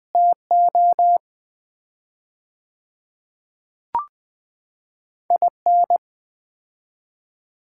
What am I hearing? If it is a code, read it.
Morse code, heard as a keyed tone, about 20 wpm, 700 Hz. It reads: TO IN
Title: 4x Word-Spacing - Sets of 3 Words - 20wpm